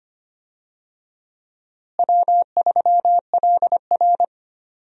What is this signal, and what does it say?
Morse code, heard as a keyed tone, about 25 wpm, 700 Hz. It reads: W3LR
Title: USA Call Signs - 4 Characters - Repeated 3x - 25wpm